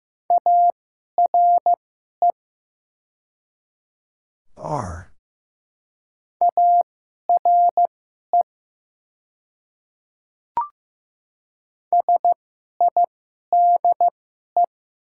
Morse code, heard as a keyed tone, about 15 wpm, spaced as for 11 wpm, 700 Hz. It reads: ARE ARE SIDE
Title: 2x Character Spacing - Top 200 Words - 15wpm